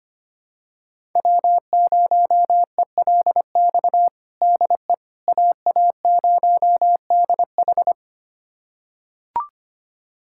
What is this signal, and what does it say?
Morse code, heard as a keyed tone, about 25 wpm, 700 Hz. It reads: W0ELX DE AA0DH
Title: USA Callsign DE Callsign - 5 Characters - 25wpm